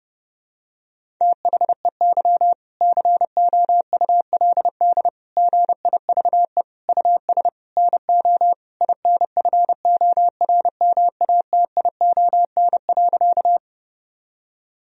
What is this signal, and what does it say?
Morse code, heard as a keyed tone, about 30 wpm, 700 Hz. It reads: THEY COULD GIVE US NO INFORMATION.
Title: Sentences from Top 1000 Words - Review Part 3 - 30wpm